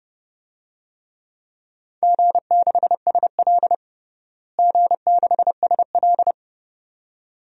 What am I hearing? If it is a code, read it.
Morse code, heard as a keyed tone, about 30 wpm, 700 Hz. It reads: G6SL G6SL
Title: DXCC-100 Call Signs - 4 Characters - Repeated 2x - 30wpm